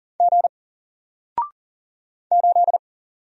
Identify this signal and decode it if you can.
Morse code, heard as a keyed tone, about 40 wpm, 700 Hz. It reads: G 8
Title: Single Letter-Number 40wpm